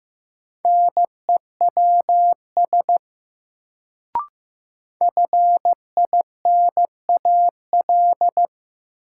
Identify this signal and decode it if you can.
Morse code, heard as a keyed tone, about 15 wpm, 700 Hz. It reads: NEWS FINAL
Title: Sentences from Top 800 Words - New Word Drill - 15wpm